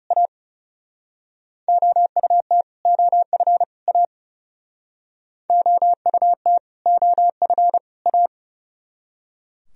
Morse code, heard as a keyed tone, about 36 wpm, 700 Hz. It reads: A OUT OF A OUT OF A
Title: Speed-Racing - Sets of 3 Words 30wpm